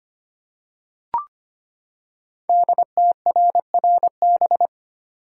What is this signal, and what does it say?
Morse code, heard as a keyed tone, about 25 wpm, 700 Hz. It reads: DTRRB
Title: Code Groups - 5 Letters - 25wpm